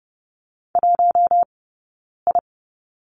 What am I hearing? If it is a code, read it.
Morse code, heard as a keyed tone, about 30 wpm, 700 Hz. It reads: 1 I
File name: Lesson 18d - Letter m - Two Characters 30wpm